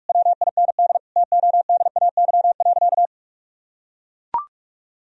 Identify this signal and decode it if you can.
Morse code, heard as a keyed tone, about 42 wpm, 700 Hz. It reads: WIND TODAY.
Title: Sentences from Top 1000 Words - Review Part 2 - 45wpm